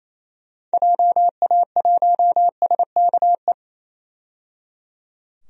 Morse code, heard as a keyed tone, about 28 wpm, 700 Hz. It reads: JA1SKE
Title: DXCC-100 Call Signs - 6 Characters - 28wpm